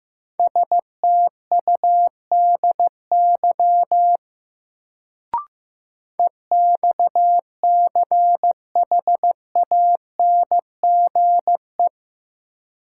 Morse code, heard as a keyed tone, about 15 wpm, 700 Hz. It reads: STUDY EXCHANGE
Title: Sets of 5 Words - New Word Drill - 15wpm